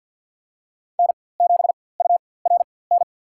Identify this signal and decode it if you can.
Morse code, heard as a keyed tone, about 52 wpm, spaced as for 23 wpm, 700 Hz. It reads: N7URN
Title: USA Call Signs - 5 Characters - 4x Character Spacing - 50wpm